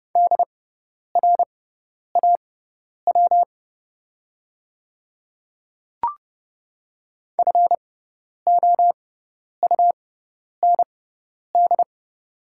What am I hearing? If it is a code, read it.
Morse code, heard as a keyed tone, about 30 wpm, spaced as for 10 wpm, 700 Hz. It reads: DRAW FOUND
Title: 6x Character Spacing - Top 300 Words - 30wpm